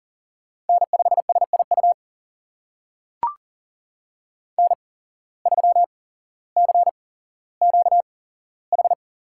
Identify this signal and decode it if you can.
Morse code, heard as a keyed tone, about 39 wpm, spaced as for 17 wpm, 700 Hz. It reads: N5SIU N3CQH
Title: USA Call Signs - 5 Characters - 8x Character Spacing - 40wpm